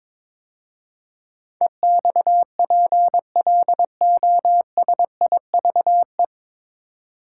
Morse code, heard as a keyed tone, about 22 wpm, 700 Hz. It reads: EXPLOSIVE
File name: 9 Letter Words - First 3 Letters Given - 22wpm